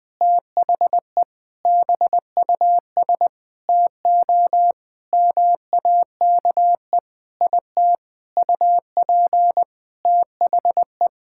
Morse code, heard as a keyed tone, about 21 wpm, 700 Hz. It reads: THE BUS TO MAKE IT UP THE
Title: Sentences from Top 1000 Words - Review Part 6 - 20wpm